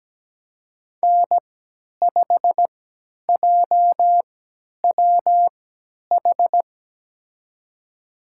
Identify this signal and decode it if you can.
Morse code, heard as a keyed tone, about 17 wpm, spaced as for 10 wpm, 700 Hz. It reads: N5JWH
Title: USA Call Signs - 3x Character Spacing - 17wpm